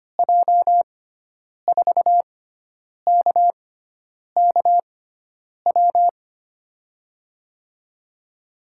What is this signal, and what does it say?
Morse code, heard as a keyed tone, about 25 wpm, spaced as for 9 wpm, 700 Hz. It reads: J4KKW
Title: USA Call Signs - 6 Characters - 6x Character Spacing - 25wpm